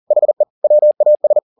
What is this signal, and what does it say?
Morse code, heard as a keyed tone, about 40 wpm, 600 Hz. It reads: HE WAS
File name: the old man and the sea 2 40wpm